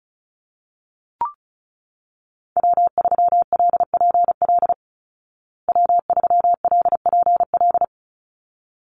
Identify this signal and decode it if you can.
Morse code, heard as a keyed tone, about 34 wpm, 700 Hz. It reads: W3LPL W3LPL
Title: Call Signs Repeated 3x - 35wpm